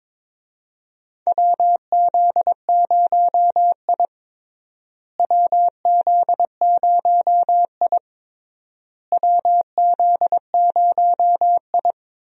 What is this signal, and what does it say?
Morse code, heard as a keyed tone, about 21 wpm, 700 Hz. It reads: WZ0I WZ0I WZ0I
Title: USA Call Signs - 4 Characters - Repeated 3x - 22wpm